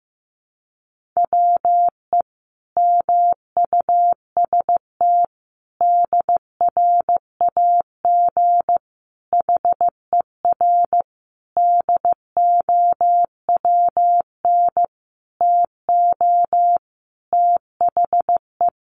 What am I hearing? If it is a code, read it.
Morse code, heard as a keyed tone, about 14 wpm, 700 Hz. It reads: WE MUST DRAG HER DOWN TO THE
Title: Sentences from Top 1500 Words 15wpm